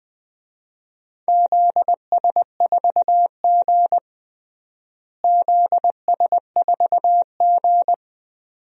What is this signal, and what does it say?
Morse code, heard as a keyed tone, about 20 wpm, 700 Hz. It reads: ZS4G ZS4G
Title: DXCC-100 Call Signs - 4 Characters - Repeated 2x - 20wpm